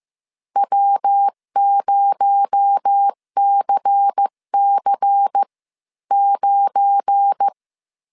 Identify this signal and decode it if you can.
Morse code, heard as a keyed tone, about 15 wpm, spaced as for 12 wpm, 800 Hz. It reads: W0CC9
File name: CW / DET / 1 9 1500C S B 14wpm 3302